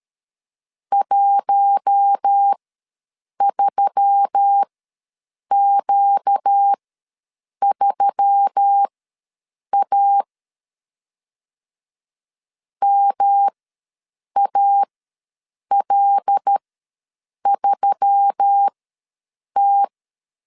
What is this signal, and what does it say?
Morse code, heard as a keyed tone, about 13 wpm, spaced as for 7 wpm, 800 Hz. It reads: 13Q3A MAL3T
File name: LEZIONI / 5 LBVC / 5 L ws 12wpm 2958